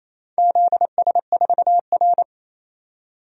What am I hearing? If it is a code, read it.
Morse code, heard as a keyed tone, about 28 wpm, 700 Hz. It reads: ZS4R
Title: DXCC-100 Call Signs - 4 Characters - 28wpm